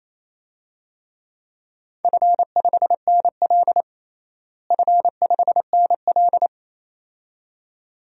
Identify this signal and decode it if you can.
Morse code, heard as a keyed tone, about 28 wpm, 700 Hz. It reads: F5NL F5NL
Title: DXCC-100 Call Signs - 4 Characters - Repeated 2x - 28wpm